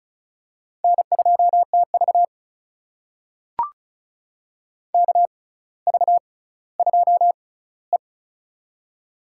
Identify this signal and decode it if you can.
Morse code, heard as a keyed tone, about 34 wpm, spaced as for 18 wpm, 700 Hz. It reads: N2TV KV2E
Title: USA Call Signs - 4 Characters - 6x Character Spacing - 35wpm